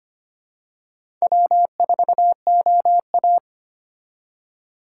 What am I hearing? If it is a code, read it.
Morse code, heard as a keyed tone, about 25 wpm, 700 Hz. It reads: W4OA
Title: USA Call Signs - 4 Characters - 4x Character Spacing - 25wpm